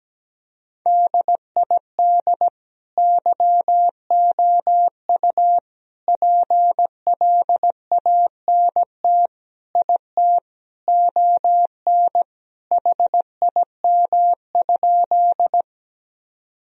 Morse code, heard as a keyed tone, about 17 wpm, 700 Hz. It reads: DID YOU PLANT IT ON HIM?
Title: Sentences from Top 1000 Words - Review Part 1 - 17wpm